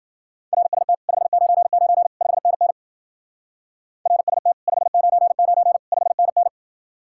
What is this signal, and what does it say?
Morse code, heard as a keyed tone, about 48 wpm, 700 Hz. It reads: RST 599 5NN RST 599 5NN
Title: Speed-Racing Common QSO Phrases - 40wpm